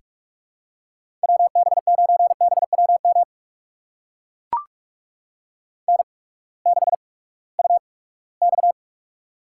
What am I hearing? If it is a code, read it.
Morse code, heard as a keyed tone, about 44 wpm, spaced as for 22 wpm, 700 Hz. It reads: WB9BWM N6UX
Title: USA Call Signs - 8x Character Spacing - 45wpm